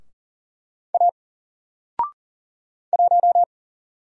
Morse code, heard as a keyed tone, about 39 wpm, 700 Hz. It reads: A 1
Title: Single Character (Letter:Number:Punctuation:Prosign) 40wpm